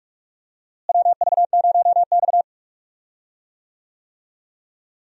Morse code, heard as a keyed tone, about 46 wpm, 700 Hz. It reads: WV0X
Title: USA Call Signs - 4 Characters - Repeated 2x - 45wpm